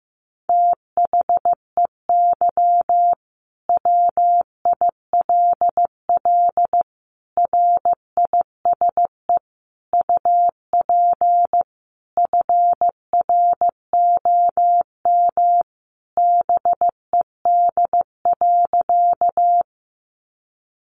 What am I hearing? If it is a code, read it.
Morse code, heard as a keyed tone, about 15 wpm, 700 Hz. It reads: THEY WILL RISE UP FROM BED.
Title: Sentences from Top 900 Words 15wpm